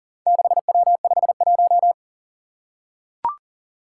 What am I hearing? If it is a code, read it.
Morse code, heard as a keyed tone, about 39 wpm, 700 Hz. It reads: 6W51
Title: Four Challenging Letter-Number Combinations 40wpm